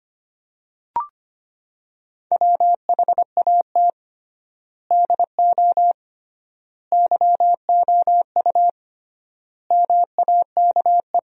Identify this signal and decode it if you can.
Morse code, heard as a keyed tone, about 25 wpm, 700 Hz. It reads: WHAT DO YOU MAKE
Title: 3x Word-Spacing - Sentences from Top 100 Words - 25wpm